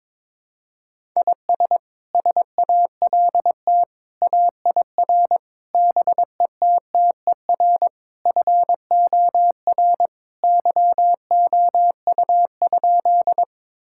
Morse code, heard as a keyed tone, about 22 wpm, 700 Hz. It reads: IS SALT AIR BETTER FOR YOU?
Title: Sentences from Top 1000 Words - Review Part 5 - 22wpm